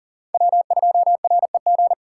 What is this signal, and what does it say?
Morse code, heard as a keyed tone, about 40 wpm, 700 Hz. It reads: W2REG
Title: USA Call Signs - 5 Characters - 6x Character Spacing - 40wpm